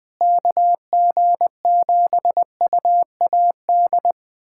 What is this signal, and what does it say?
Morse code, heard as a keyed tone, about 20 wpm, 700 Hz. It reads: KG7UAD
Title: USA Callsign DE Callsign 20wpm